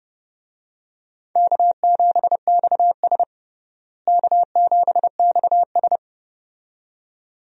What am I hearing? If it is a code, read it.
Morse code, heard as a keyed tone, about 30 wpm, 700 Hz. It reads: K7XS K7XS
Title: USA Call Signs - 4 Characters - Repeated 2x - 30wpm